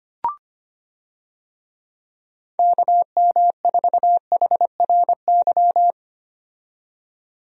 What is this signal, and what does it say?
Morse code, heard as a keyed tone, about 25 wpm, 700 Hz. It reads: KM4HRY
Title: USA Call Signs - 6 Characters - 25wpm